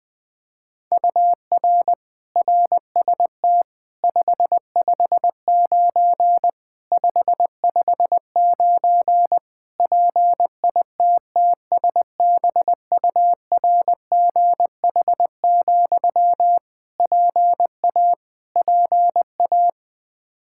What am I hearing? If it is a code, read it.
Morse code, heard as a keyed tone, about 20 wpm, 700 Hz. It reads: UR RST 559 559 PITTSBURGH, PA PA
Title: Practice QSO - Exchange 1 - Beginning - 20wpm